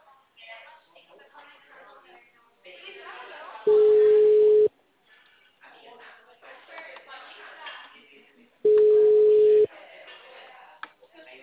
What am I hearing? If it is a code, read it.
Ringback tone (Europe, 425 Hz), with speech.
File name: out- -2004- - - .196